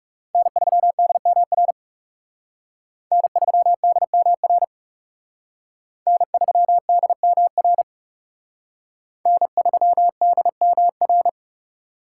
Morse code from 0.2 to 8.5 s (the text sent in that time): N3DMR N3DMR N3DMR